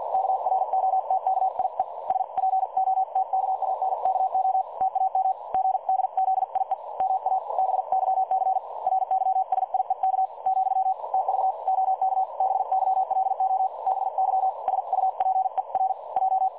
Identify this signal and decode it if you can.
Morse code, heard as a keyed tone, about 50 wpm, spaced as for 44 wpm, 760 Hz. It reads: KROTKIE SPOTKANKO TAM GDZIE MA BYC RYSIEK , TV OK BYLOBY SUPER J
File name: dlazaawansowanych / nagraniazpasma / websdr recording 3565.0kHz